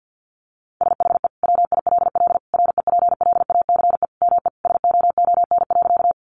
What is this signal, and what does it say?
Morse code, heard as a keyed tone, about 48 wpm, 700 Hz. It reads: SHE WILL REPLACE ME SOON.
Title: Sentences from Top 1400 Words 50wpm